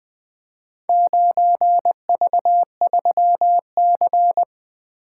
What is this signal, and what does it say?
Morse code, heard as a keyed tone, about 20 wpm, 700 Hz. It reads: 9V3C
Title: Four Letter-Number Combinations - 20wpm